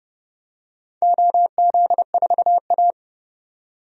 Morse code, heard as a keyed tone, about 30 wpm, 700 Hz. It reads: OZ4A
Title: DXCC-100 Call Signs - 4 Characters - 30wpm